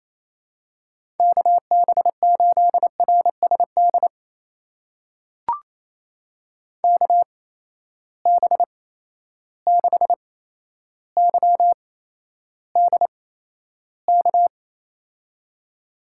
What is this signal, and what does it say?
Morse code, heard as a keyed tone, about 28 wpm, spaced as for 12 wpm, 700 Hz. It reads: KB8RSD KB6YDK
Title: USA Call Signs - 6 Characters - 8x Character Spacing - 28wpm